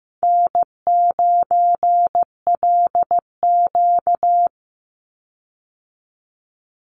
Morse code, heard as a keyed tone, about 15 wpm, 700 Hz. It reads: N9LQ
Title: Easy Call Signs Repeated 2x - 15wpm